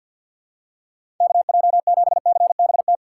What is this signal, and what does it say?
Morse code, heard as a keyed tone, about 48 wpm, 700 Hz. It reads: KJ7CBT
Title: USA Call Signs - 6 Characters - 3x Character Spacing - 50wpm